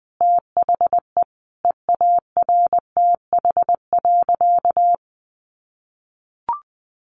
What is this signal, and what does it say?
Morse code, heard as a keyed tone, about 20 wpm, 700 Hz. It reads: THE EARTH.
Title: A Sound Of Thunder 20wpm